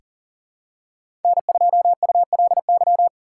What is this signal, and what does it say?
Morse code, heard as a keyed tone, about 40 wpm, 700 Hz. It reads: N2ULY
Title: USA Call Signs - 5 Characters - 2x Character Spacing - 40wpm